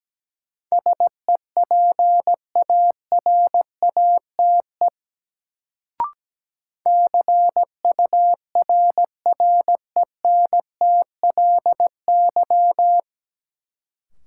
Morse code, heard as a keyed tone, about 17 wpm, 700 Hz. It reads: SEPARATE CURRENTLY
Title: Sentences from Top 1000 Words - Review Part 7 - New Word Drill - 17wpm